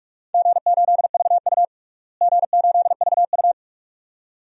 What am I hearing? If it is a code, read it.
Morse code, heard as a keyed tone, about 44 wpm, 700 Hz. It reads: G8VU G8VU
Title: DXCC-100 Call Signs - 4 Characters - Repeated 2x - 45wpm